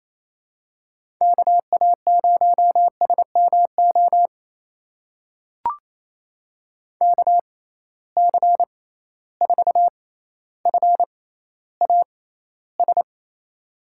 Morse code, heard as a keyed tone, about 28 wpm, spaced as for 14 wpm, 700 Hz. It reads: KA0SMO KC4FAS
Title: USA Call Signs - 6 Characters - 6x Character Spacing - 28wpm